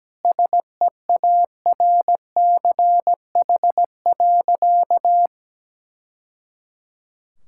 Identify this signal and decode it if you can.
Morse code, heard as a keyed tone, about 17 wpm, 700 Hz. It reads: SEARCH.
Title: Sentences from Top 1000 Words - Review Part 5 - 17wpm